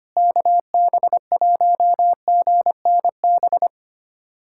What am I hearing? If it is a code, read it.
Morse code, heard as a keyed tone, about 25 wpm, 700 Hz. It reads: KB1GNB
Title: USA Callsign DE Callsign 25wpm